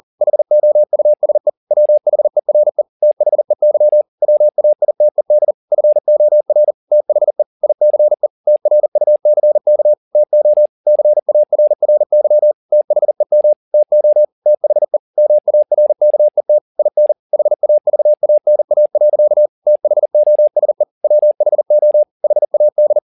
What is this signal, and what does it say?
Morse code, heard as a keyed tone, about 40 wpm, 600 Hz. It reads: HOUSE WHERE THEY WAITED FOR THE ICE TRUCK TO CARRY THEM TO THE MARKET IN HAVANA. THOSE WHO HAD
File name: the old man and the sea 1 40wpm